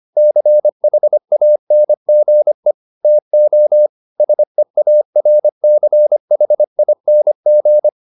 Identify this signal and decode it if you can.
Morse code, heard as a keyed tone, about 25 wpm, 600 Hz. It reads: CHANGE TO SEARCHING